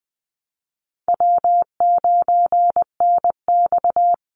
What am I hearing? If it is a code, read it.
Morse code, heard as a keyed tone, about 20 wpm, 700 Hz. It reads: W9NX
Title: Easy Call Signs - 20wpm